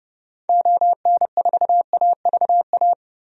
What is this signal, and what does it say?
Morse code, heard as a keyed tone, about 31 wpm, 700 Hz. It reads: ON4AVA
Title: DXCC-100 Call Signs - 6 Characters - Repeated 3x - 30wpm